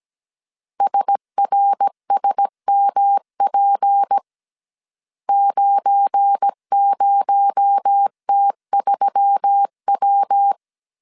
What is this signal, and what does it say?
Morse code, heard as a keyed tone, about 17 wpm, 800 Hz. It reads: SRSMP 90T3W
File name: CW / DET / 1 9 1500C S A 16wpm 2828